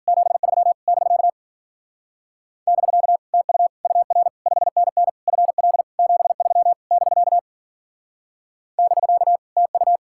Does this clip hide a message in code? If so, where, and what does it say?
Yes: Morse code from 0.1 to 10.0 s, heard as a keyed tone, about 52 wpm, 700 Hz. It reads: 73 <BK> <BK> TU UR 5NN FL 73 <BK> <BK> TU